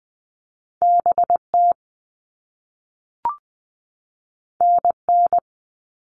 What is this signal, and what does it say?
Morse code, heard as a keyed tone, about 20 wpm, 700 Hz. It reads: BT NN
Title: All Cut Numbers - 2-Digit 20wpm